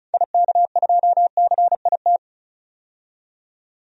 Morse code, heard as a keyed tone, about 34 wpm, 700 Hz. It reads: IK2CIT
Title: DXCC-100 Call Signs - 6 Characters - 35wpm